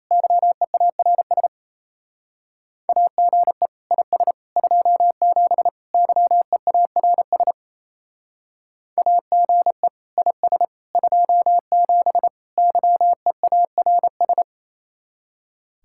Morse code, heard as a keyed tone, about 34 wpm, 700 Hz. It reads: YEARS AGE IS 27 YEARS AGE IS 27 YEARS
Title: Speed-Racing Common QSO Phrases Encore - 28wpm